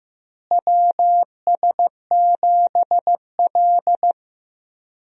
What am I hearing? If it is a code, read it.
Morse code, heard as a keyed tone, about 16 wpm, 700 Hz. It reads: WS7L
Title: USA Call Signs - 4 Characters - Repeated 2x - 15wpm